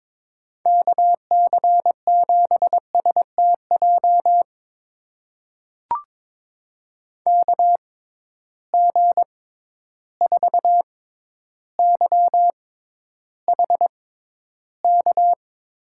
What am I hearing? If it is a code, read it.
Morse code, heard as a keyed tone, about 22 wpm, spaced as for 11 wpm, 700 Hz. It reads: KC7STJ KG4YHK
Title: USA Call Signs - 6 Characters - 6x Character Spacing - 22wpm